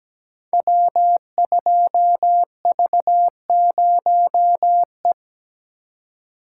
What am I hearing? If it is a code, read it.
Morse code, heard as a keyed tone, about 17 wpm, 700 Hz. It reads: W2V0E
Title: Code Groups - 5 Letter:Numbers - 17wpm